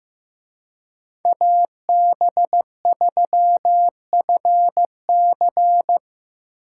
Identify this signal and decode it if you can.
Morse code, heard as a keyed tone, about 18 wpm, 700 Hz. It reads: AB3FC
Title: Speed Racing - USA 2x2 Call Signs 15wpm